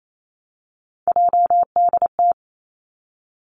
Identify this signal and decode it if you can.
Morse code, heard as a keyed tone, about 28 wpm, 700 Hz. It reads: JDT
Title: Three Challenging Letter-Number Combinations 28wpm